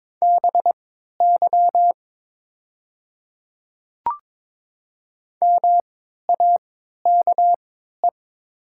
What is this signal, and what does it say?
Morse code, heard as a keyed tone, about 22 wpm, spaced as for 12 wpm, 700 Hz. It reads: BY MAKE
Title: 3x Character Spacing - Top 100 Words - 22wpm